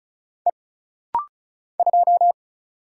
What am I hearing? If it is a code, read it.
Morse code, heard as a keyed tone, about 36 wpm, 700 Hz. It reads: E 2
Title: Single Letter-Number - Rapid-Fire 35wpm